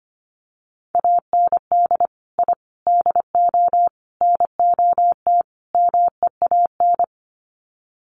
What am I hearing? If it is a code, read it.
Morse code, heard as a keyed tone, about 25 wpm, 700 Hz. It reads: AND I DO NOT MEAN
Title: Sets of 5 Words Encore2 25wpm